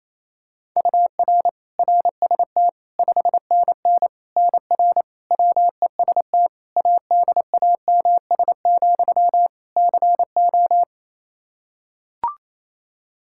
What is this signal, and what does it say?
Morse code, heard as a keyed tone, about 28 wpm, 700 Hz. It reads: UR RST 5NN NR WEST ADAMS, CO
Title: Practice QSO - Exchange 1 - Beginning - 28wpm